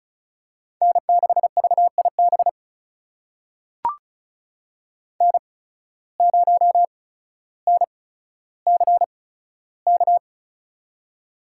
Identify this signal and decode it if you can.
Morse code, heard as a keyed tone, about 34 wpm, spaced as for 15 wpm, 700 Hz. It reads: N6VIB N0NCK
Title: USA Call Signs - 5 Characters - 8x Character Spacing - 35wpm